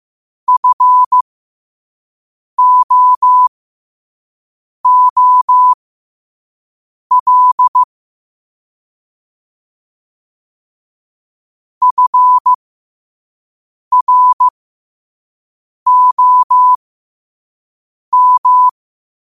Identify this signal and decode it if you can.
Morse code, heard as a keyed tone, about 15 wpm, spaced as for 5 wpm, 1000 Hz. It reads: FOOL FROM
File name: qotdfiles / 5wpm